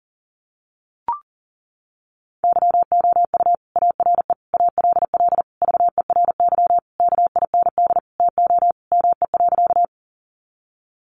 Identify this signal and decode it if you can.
Morse code, heard as a keyed tone, about 40 wpm, 700 Hz. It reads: YOU ARE ALL VERY KIND TO ME.
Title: Sentences from Top 200 Words 40wpm